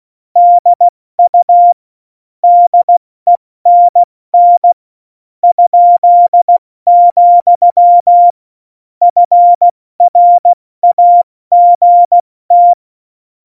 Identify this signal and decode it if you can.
Morse code, heard as a keyed tone, about 16 wpm, spaced as for 13 wpm, 700 Hz. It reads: DU DENN?, FRAGT
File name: Morsekurs / Witze / MP3, Witz 24 April 2018, 70 F80